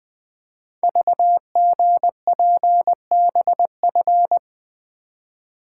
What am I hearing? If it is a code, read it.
Morse code, heard as a keyed tone, about 20 wpm, 700 Hz. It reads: VGPBF
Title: Code Groups - 5 Letters - 20wpm